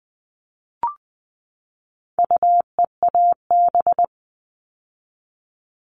Morse code, heard as a keyed tone, about 20 wpm, 700 Hz. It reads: UEAB
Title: All Cut Numbers - 4-Digit 20wpm